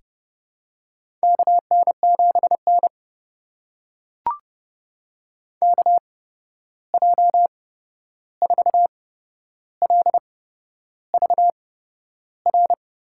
Morse code, heard as a keyed tone, about 30 wpm, spaced as for 11 wpm, 700 Hz. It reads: KN7N KJ4LVR
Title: USA Call Signs - 8x Character Spacing - 30wpm